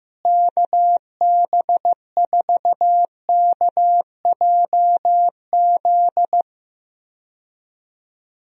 Morse code, heard as a keyed tone, about 15 wpm, 700 Hz. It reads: KB4KJZ
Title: USA Call Signs - 6 Characters - Repeated 3x - 15wpm